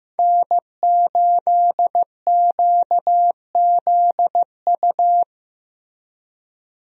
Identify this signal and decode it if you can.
Morse code, heard as a keyed tone, about 15 wpm, 700 Hz. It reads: N8QZU
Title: USA Call Signs - 5 Characters - Repeated 3x - 15wpm